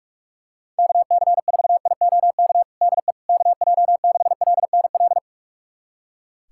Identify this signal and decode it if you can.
Morse code, heard as a keyed tone, about 46 wpm, 700 Hz. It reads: KC4IOK DE KJ6LNL
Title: USA Callsign DE Callsign - 6 Characters - 45wpm